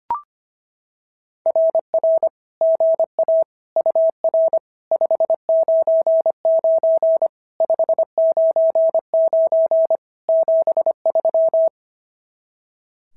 Morse code, heard as a keyed tone, about 24 wpm, 650 Hz. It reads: RR GA UR 599 599 73
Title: SOTA QSO Practice 25wpm